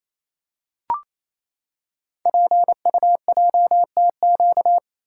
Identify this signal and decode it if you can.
Morse code, heard as a keyed tone, about 28 wpm, 700 Hz. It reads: PUJTQ
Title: Code Groups - 5 Letter:Numbers - 28wpm